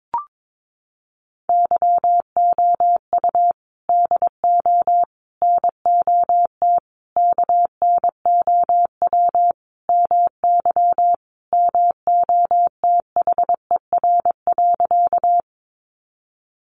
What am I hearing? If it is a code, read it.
Morse code, heard as a keyed tone, about 22 wpm, 700 Hz. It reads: YOU DO NOT KNOW MY MOTHER.
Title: Sentences from Top 600 Words 22wpm